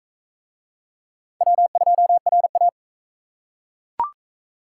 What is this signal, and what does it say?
Morse code, heard as a keyed tone, about 37 wpm, 700 Hz. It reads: W2RA